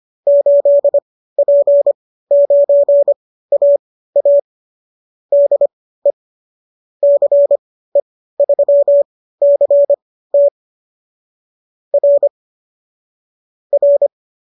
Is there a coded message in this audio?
Yes: Morse code from 0.3 to 14.1 s, heard as a keyed tone, about 25 wpm, spaced as for 15 wpm, 570 Hz. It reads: 8P9AA DE CE3CT R R